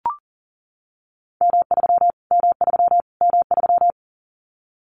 Morse code, heard as a keyed tone, about 40 wpm, 700 Hz. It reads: M3 M3 M3
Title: Callsign Prefixes for Top 25 DXCC - Repeated 3x - 40wpm